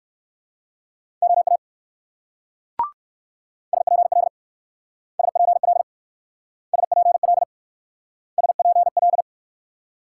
Morse code, heard as a keyed tone, about 64 wpm, 700 Hz. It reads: OA SPL SPL SPL SPL